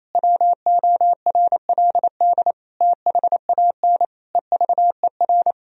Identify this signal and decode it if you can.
Morse code, heard as a keyed tone, about 28 wpm, 700 Hz. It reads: WORLD THAN EVER
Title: Sentences from Top 1000 Words - Review Part 3 - 28wpm